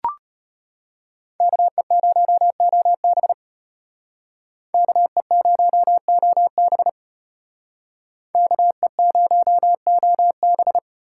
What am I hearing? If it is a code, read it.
Morse code, heard as a keyed tone, about 32 wpm, 700 Hz. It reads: KE0OB KE0OB KE0OB